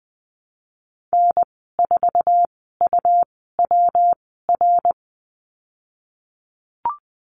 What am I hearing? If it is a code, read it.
Morse code, heard as a keyed tone, about 20 wpm, spaced as for 14 wpm, 700 Hz. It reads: N4UWR
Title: 147 Lesson 24g - Letter y - Call Signs 20wpm